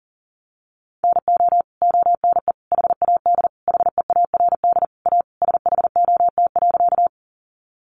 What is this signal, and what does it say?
Morse code, heard as a keyed tone, about 40 wpm, 700 Hz. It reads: NO ONE HAD HEARD A SHOT.
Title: Sentences from Top 1100 Words 40wpm